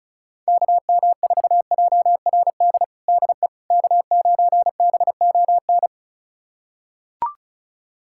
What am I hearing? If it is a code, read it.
Morse code, heard as a keyed tone, about 36 wpm, 700 Hz. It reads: KM4JRD DE K9BON
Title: USA Callsign DE Callsign 35wpm